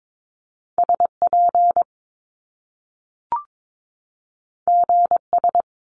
Morse code, heard as a keyed tone, about 22 wpm, 700 Hz. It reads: SP GS
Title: Top Two Letter Combinations in Words 22wpm